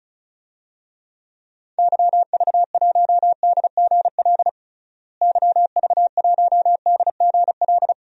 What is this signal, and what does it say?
Morse code, heard as a keyed tone, about 36 wpm, 700 Hz. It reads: YV1DGL YV1DGL
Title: DXCC-100 Call Signs - 6 Characters - Repeated 3x - 35wpm